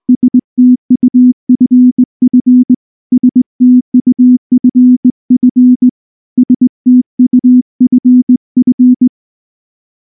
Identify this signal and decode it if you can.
Morse code, heard as a keyed tone, about 19 wpm, 250 Hz. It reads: STUFF STUFF STUFF
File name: morse / morse-training-20WPM-3